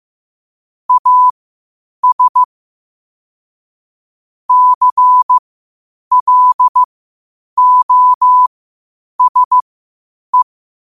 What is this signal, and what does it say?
Morse code, heard as a keyed tone, about 15 wpm, spaced as for 8 wpm, 1000 Hz. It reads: AS CLOSE
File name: qotdfiles / 8wpm